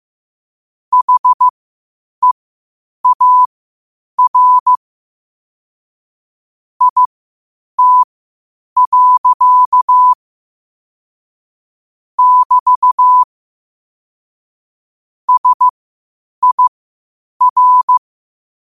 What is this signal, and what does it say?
Morse code, heard as a keyed tone, about 15 wpm, spaced as for 8 wpm, 1000 Hz. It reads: HEAR IT. <BT> SIR
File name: 8wpm